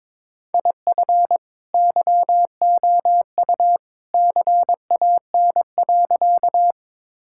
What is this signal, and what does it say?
Morse code, heard as a keyed tone, about 21 wpm, 700 Hz. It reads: IF YOU CAN.
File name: practice / 22wpm / Sentences with 2-Word Phrases - 22wpm